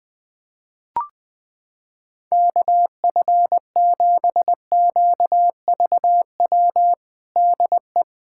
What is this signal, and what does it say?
Morse code, heard as a keyed tone, about 20 wpm, 700 Hz. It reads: KF7QVW DE
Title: USA Callsign DE Callsign 20wpm